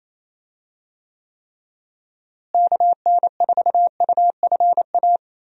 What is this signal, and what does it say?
Morse code, heard as a keyed tone, about 28 wpm, 700 Hz. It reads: KN4UFA
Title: USA Call Signs - 6 Characters - Repeated 3x - 28wpm